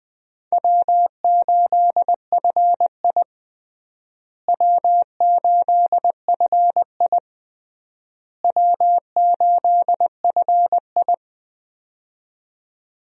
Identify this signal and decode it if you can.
Morse code, heard as a keyed tone, about 20 wpm, 700 Hz. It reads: W8FI W8FI W8FI
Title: USA Call Signs - 4 Characters - Repeated 3x - 20wpm